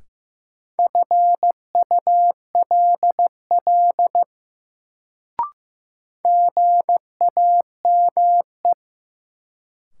Morse code, heard as a keyed tone, about 15 wpm, 700 Hz. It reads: FULL GAME
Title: Sentences from Top 500 Words - New Word Drill - 15wpm